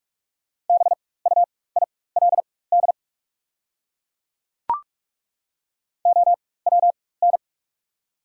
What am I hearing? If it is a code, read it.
Morse code, heard as a keyed tone, about 43 wpm, spaced as for 21 wpm, 690 Hz. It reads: BUILD OWN
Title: 4x Character Spacing - Top 500 Words - 45wpm